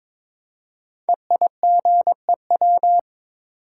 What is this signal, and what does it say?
Morse code, heard as a keyed tone, about 22 wpm, 700 Hz. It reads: EIGEW
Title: Code Groups - 5 Letters - 22wpm